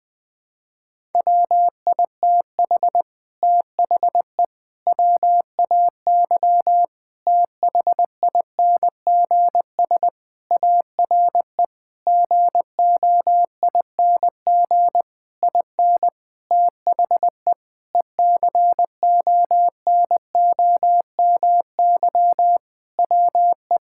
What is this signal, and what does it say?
Morse code, heard as a keyed tone, about 20 wpm, 700 Hz. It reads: WITH THE WAY THINGS ARE GOING IN THE ECONOMY WE
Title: Sentences with 8-Word Phrases - Spoken Beginning - 20wpm